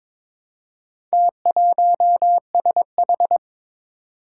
Morse code, heard as a keyed tone, about 23 wpm, 700 Hz. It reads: T1SH
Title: Four Challenging Letter-Number Combinations 22wpm